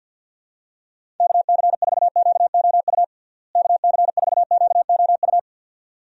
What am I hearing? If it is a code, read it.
Morse code, heard as a keyed tone, about 46 wpm, 700 Hz. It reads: KC4QOU KC4QOU
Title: USA Call Signs - 6 Characters - Repeated 2x - 50wpm